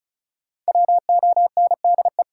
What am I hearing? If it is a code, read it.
Morse code, heard as a keyed tone, about 34 wpm, 700 Hz. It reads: WONDE
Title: Sentences from Top 500 Words - New Word Drill - 35wpm